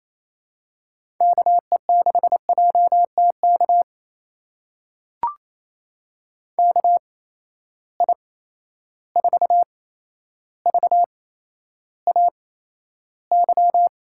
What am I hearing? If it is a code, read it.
Morse code, heard as a keyed tone, about 28 wpm, spaced as for 12 wpm, 700 Hz. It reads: KE6JTK KI4VAY
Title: USA Call Signs - 6 Characters - 8x Character Spacing - 28wpm